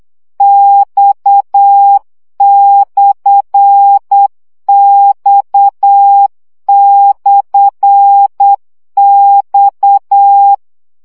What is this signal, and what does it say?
Morse code, heard as a keyed tone, about 8 wpm, 800 Hz. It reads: X/X/X
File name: CW gyakorlas x per 7 baud